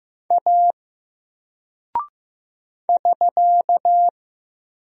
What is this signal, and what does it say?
Morse code, heard as a keyed tone, about 15 wpm, 700 Hz. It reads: A <SK>
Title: Single Character (Letter:Number:Punctuation:Prosign) 15wpm